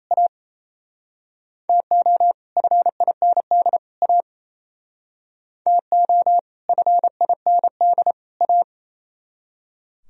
Morse code, heard as a keyed tone, about 35 wpm, 700 Hz. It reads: A TO FIND A TO FIND A
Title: Speed-Racing - Sets of 3 Words 28wpm